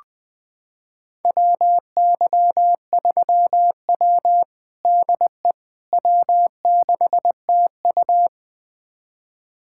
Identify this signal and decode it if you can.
Morse code, heard as a keyed tone, about 21 wpm, 700 Hz. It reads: WY3W DE W6TU
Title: USA Callsign DE Callsign - 4 Characters - 20wpm